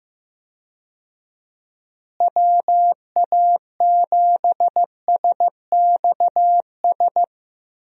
Morse code, heard as a keyed tone, about 15 wpm, 700 Hz. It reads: WA7SXS
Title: USA Call Signs - 6 Characters - Repeated 3x - 15wpm